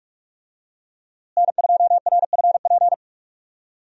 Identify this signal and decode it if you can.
Morse code, heard as a keyed tone, about 46 wpm, 700 Hz. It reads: N2RFP
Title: USA Call Signs - 5 Characters - Repeated 3x - 45wpm